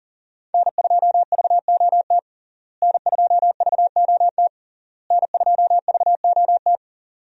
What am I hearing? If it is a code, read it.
Morse code, heard as a keyed tone, about 40 wpm, 700 Hz. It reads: N2VOT N2VOT N2VOT
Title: USA Call Signs - 5 Characters - Repeated 3x - 40wpm